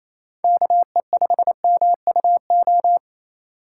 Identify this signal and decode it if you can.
Morse code, heard as a keyed tone, about 28 wpm, 700 Hz. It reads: KE5MUO
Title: USA Call Signs - 3x Character Spacing - 28wpm